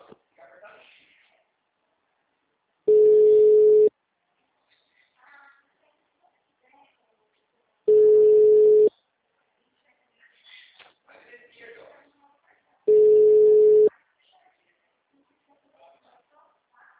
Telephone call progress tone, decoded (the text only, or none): ringback tone